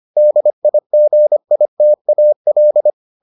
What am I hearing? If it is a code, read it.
Morse code, heard as a keyed tone, about 25 wpm, 600 Hz. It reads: DIGITAL